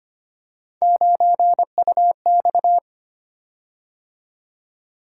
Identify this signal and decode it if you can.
Morse code, heard as a keyed tone, about 25 wpm, 700 Hz. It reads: 9UX
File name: Three Letter-Number Combinations - 25wpm